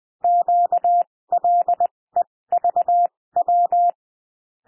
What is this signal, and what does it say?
Morse code, heard as a keyed tone, about 20 wpm, spaced as for 16 wpm, 700 Hz. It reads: QLEVW
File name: cwmp3 / L1 SOUND LETTERS 20 wpm